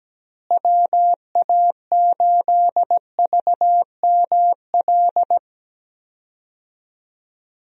Morse code, heard as a keyed tone, about 17 wpm, 700 Hz. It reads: WA8VML
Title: USA Call Signs - 6 Characters - Repeated 3x - 17wpm